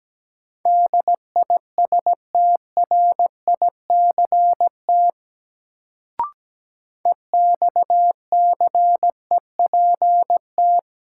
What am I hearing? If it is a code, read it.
Morse code, heard as a keyed tone, about 17 wpm, 700 Hz. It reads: DISTRICT EXCEPT
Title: Sentences from Top 1600 Words - New Word Drill - 17wpm